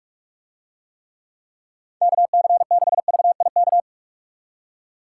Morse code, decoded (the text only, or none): KC6VIK